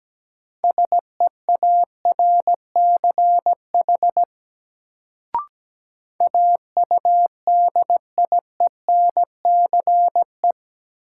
Morse code, heard as a keyed tone, about 17 wpm, 700 Hz. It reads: SEARCH AUDIENCE
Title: Sentences from Top 800 Words - New Word Drill - 17wpm